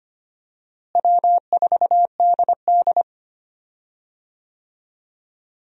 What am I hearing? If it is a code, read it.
Morse code, heard as a keyed tone, about 25 wpm, 700 Hz. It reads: W4DD
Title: USA Call Signs - 4 Characters - Repeated 3x - 25wpm